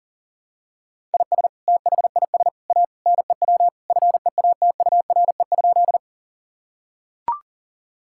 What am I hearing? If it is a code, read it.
Morse code, heard as a keyed tone, about 40 wpm, 700 Hz. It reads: IS THIS A NEW FEATURE?
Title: Sentences from Top 1000 Words - Review Part 5 - 40wpm